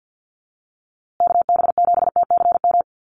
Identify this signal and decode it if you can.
Morse code, heard as a keyed tone, about 50 wpm, 700 Hz. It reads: KB7TCM
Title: USA Call Signs Repeated 2x 50wpm